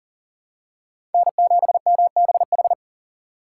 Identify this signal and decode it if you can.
Morse code, heard as a keyed tone, about 37 wpm, 700 Hz. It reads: N7MBH
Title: USA Call Signs - 3x Character Spacing - 40wpm